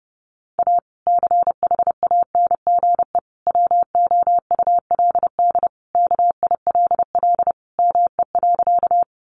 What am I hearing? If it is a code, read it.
Morse code, heard as a keyed tone, about 30 wpm, 700 Hz. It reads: A CHANGE WOULD KILL ME.
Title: Sentences from Top 1400 Words 30wpm